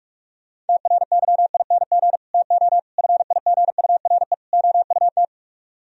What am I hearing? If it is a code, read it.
Morse code, heard as a keyed tone, about 58 wpm, 700 Hz. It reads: TRYING TO FIGURE OUT